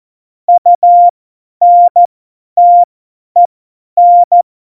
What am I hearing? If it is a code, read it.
Morse code, heard as a keyed tone, about 14 wpm, spaced as for 10 wpm, 700 Hz. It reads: UNTEN
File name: Morsekurs / Witze / Witz 24. November 2020, 50 F70